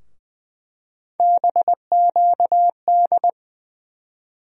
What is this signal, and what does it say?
Morse code, heard as a keyed tone, about 21 wpm, 700 Hz. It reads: BQD
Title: USA Call Sign Postfixes - 3 Letters 20wpm